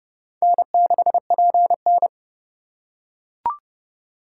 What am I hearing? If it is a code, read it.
Morse code, heard as a keyed tone, about 30 wpm, 700 Hz. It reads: N6PN
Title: USA Call Signs - 2x Character Spacing - 30wpm